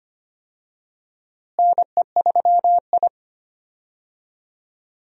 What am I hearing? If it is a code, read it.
Morse code, heard as a keyed tone, about 25 wpm, 700 Hz. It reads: NE3I
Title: USA Call Signs - 4 Characters - Repeated 3x - 25wpm